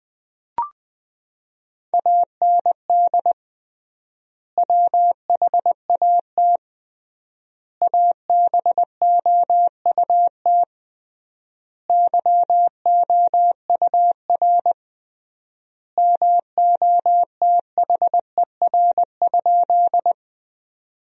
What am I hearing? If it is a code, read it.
Morse code, heard as a keyed tone, about 20 wpm, 700 Hz. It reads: AND WHAT ABOUT YOUR MOTHER?
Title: 3x Word-Spacing - Sentences from Top 200 Words - 20wpm